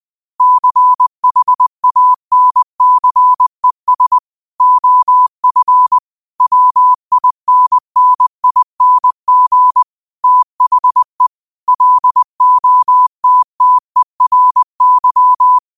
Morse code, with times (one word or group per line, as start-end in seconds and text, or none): 0.4-4.2 CHANCES
4.6-6.0 OF
6.4-9.8 WINNING
10.2-11.3 THE
11.7-15.6 LOTTERY